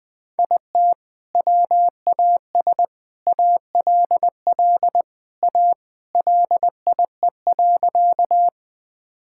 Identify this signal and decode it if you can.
Morse code, heard as a keyed tone, about 20 wpm, 700 Hz. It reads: IT WAS ALL A LIE.
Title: Sentences from Top 1600 Words 20wpm